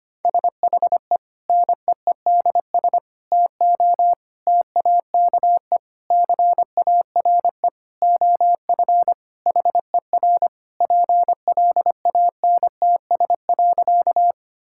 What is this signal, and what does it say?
Morse code, heard as a keyed tone, about 25 wpm, 700 Hz. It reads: SHE NEEDS TO TAKE CARE OF HER PLANTS.
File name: practice / 25wpm / Sentences with 4-Word Phrases - 25wpm